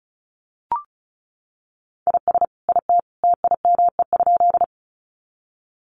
Morse code, heard as a keyed tone, about 34 wpm, 700 Hz. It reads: IS IT TIME?
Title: Sentences from Top 100 Words 35wpm